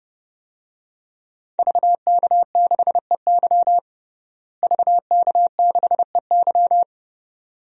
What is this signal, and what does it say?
Morse code, heard as a keyed tone, about 30 wpm, 700 Hz. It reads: VK6EY VK6EY
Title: DXCC-100 Call Signs - 5 Characters - Repeated 2x - 30wpm